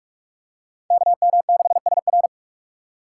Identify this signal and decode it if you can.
Morse code, heard as a keyed tone, about 47 wpm, 700 Hz. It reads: KM6SR
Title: USA Call Signs - 5 Characters - 6x Character Spacing - 45wpm